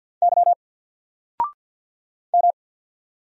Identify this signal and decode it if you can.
Morse code, heard as a keyed tone, about 50 wpm, 700 Hz. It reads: Y M
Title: Single Letter-Number 50wpm